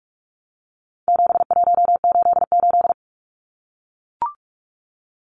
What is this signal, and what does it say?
Morse code, heard as a keyed tone, about 44 wpm, 700 Hz. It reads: 7188
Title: Numbers - 4-Digit 45wpm